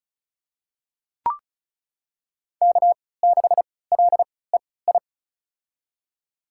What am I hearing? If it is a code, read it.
Morse code, heard as a keyed tone, about 34 wpm, spaced as for 20 wpm, 700 Hz. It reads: K6LEI
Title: USA Call Signs - 5 Characters - 3x Character Spacing - 35wpm